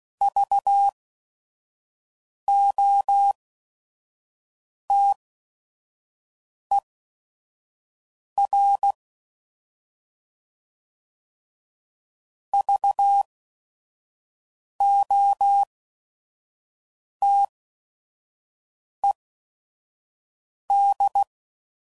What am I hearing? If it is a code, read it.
Morse code, heard as a keyed tone, about 16 wpm, spaced as for 5 wpm, 800 Hz. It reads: VOTER VOTED
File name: Prefixes / By First Letter / Vs